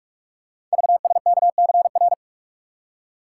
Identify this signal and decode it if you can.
Morse code, heard as a keyed tone, about 44 wpm, 700 Hz. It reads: VSKCR
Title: Code Groups - 5 Letters - 45wpm